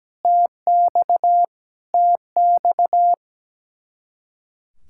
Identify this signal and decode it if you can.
Morse code, heard as a keyed tone, about 17 wpm, 700 Hz. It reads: TX TX
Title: Speed-Racing Common QSO Phrases Encore - 17wpm